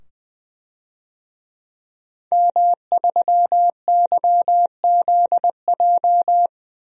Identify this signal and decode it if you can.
Morse code, heard as a keyed tone, about 20 wpm, 700 Hz. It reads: M3YZJ
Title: DXCC-100 Call Signs - 5 Characters - 20wpm